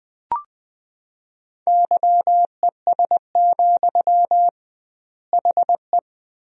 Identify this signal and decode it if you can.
Morse code, heard as a keyed tone, about 20 wpm, 700 Hz. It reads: YES, HE
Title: 2x Word-Spacing - Sentences from Top 500 Words - 20wpm